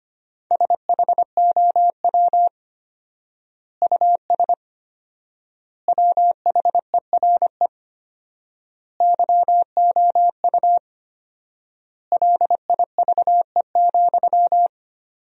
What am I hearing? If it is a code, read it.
Morse code, heard as a keyed tone, about 25 wpm, 700 Hz. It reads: SHOW US WHERE YOU LIVE,
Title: 4x Word-Spacing - Sentences from Top 200 Words - 25wpm